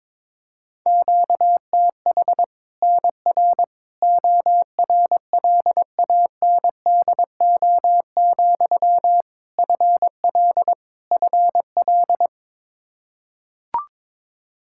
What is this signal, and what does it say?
Morse code, heard as a keyed tone, about 22 wpm, 700 Hz. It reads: QTH NR ORLANDO, FL FL
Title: Practice QSO - Exchange 1 - Beginning - 22wpm